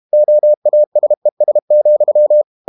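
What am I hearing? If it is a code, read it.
Morse code, heard as a keyed tone, about 32 wpm, 600 Hz. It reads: OASES,
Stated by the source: Morse code practice at 32 wpm from CW med Gnister, reading Ken Follett's novel Never in English.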